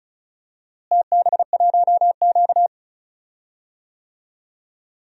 Morse code, heard as a keyed tone, about 36 wpm, 700 Hz. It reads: TB1Q
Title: DXCC-100 Call Signs - 4 Characters - Repeated 3x - 35wpm